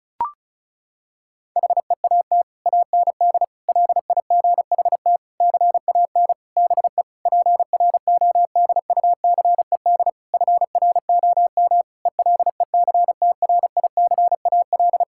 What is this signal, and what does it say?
Morse code, heard as a keyed tone, about 34 wpm, 700 Hz. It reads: HEAT AND LIGHT CAN BE PRODUCED FROM ELECTRICAL
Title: Sentences from Top 1000 Words - Review Part 6 - 35wpm